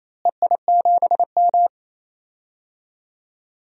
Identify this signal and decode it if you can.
Morse code, heard as a keyed tone, about 28 wpm, 700 Hz. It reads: EI7M
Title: DXCC-100 Call Signs - 4 Characters - Repeated 2x - 28wpm